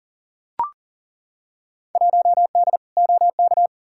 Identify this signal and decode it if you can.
Morse code, heard as a keyed tone, about 40 wpm, 700 Hz. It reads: 1D OK
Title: Field Day Exchanges - 40wpm